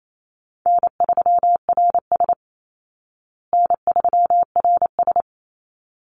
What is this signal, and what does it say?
Morse code, heard as a keyed tone, about 28 wpm, 700 Hz. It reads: N3RS N3RS
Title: Easy Call Signs Repeated 2x - 28wpm